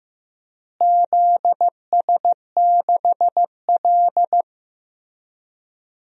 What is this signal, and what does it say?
Morse code, heard as a keyed tone, about 15 wpm, 700 Hz. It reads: ZS6L
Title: DXCC-100 Call Signs - 4 Characters - Repeated 2x - 15wpm